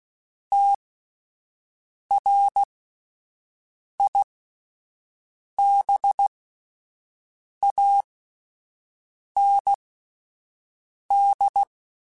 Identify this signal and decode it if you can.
Morse code, heard as a keyed tone, about 16 wpm, spaced as for 5 wpm, 800 Hz. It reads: TRIBAND
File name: (spaced) / QSOs Spaced-mp3s / WD0BGP 1 @10 sp